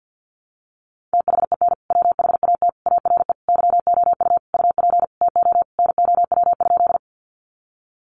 Morse code, heard as a keyed tone, about 50 wpm, 700 Hz. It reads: THEN WHAT ARE YOU UP TO NOW?
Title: Sentences from Top 100 Words 50wpm